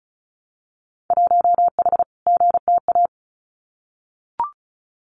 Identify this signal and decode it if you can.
Morse code, heard as a keyed tone, about 36 wpm, 700 Hz. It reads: 1H GTA
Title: Winter Field Day Exchanges Encore 35wpm